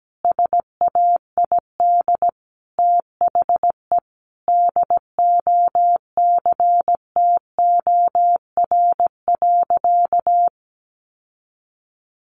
Morse code, heard as a keyed tone, about 17 wpm, 700 Hz. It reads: SAID THE DOCTOR.
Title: Sentences from Top 1500 Words 17wpm